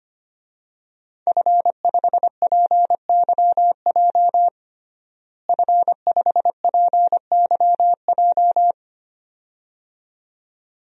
Morse code, heard as a keyed tone, about 25 wpm, 700 Hz. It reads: F5PYJ F5PYJ
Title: DXCC-100 Call Signs - 5 Characters - Repeated 2x - 25wpm